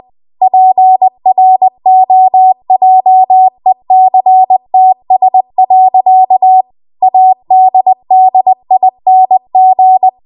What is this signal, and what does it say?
Morse code, heard as a keyed tone, about 20 wpm, 750 Hz. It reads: PROJECTS. ADDING